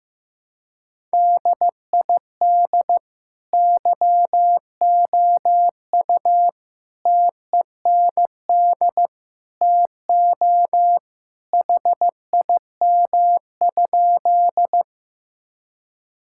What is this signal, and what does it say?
Morse code, heard as a keyed tone, about 15 wpm, 700 Hz. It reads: DID YOU TEND TO HIM?
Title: Sentences from Top 1000 Words - Review Part 5 - 15wpm